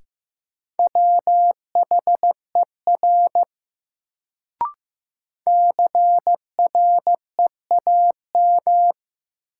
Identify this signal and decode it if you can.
Morse code, heard as a keyed tone, about 15 wpm, 700 Hz. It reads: WHER CREAM
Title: Sentences from Top 1500 Words - New Word Drill Alt - 15wpm